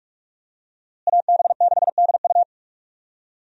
Morse code, heard as a keyed tone, about 41 wpm, 700 Hz. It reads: AB6DU
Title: USA Call Signs - 5 Characters - Repeated 2x - 45wpm